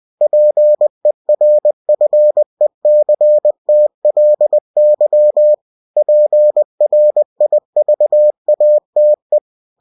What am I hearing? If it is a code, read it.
Morse code, heard as a keyed tone, about 20 wpm, 600 Hz. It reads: PERFECTLY PRIVATE
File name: cw-books / winnie-the-pooh / winnie-20wpm-page-0082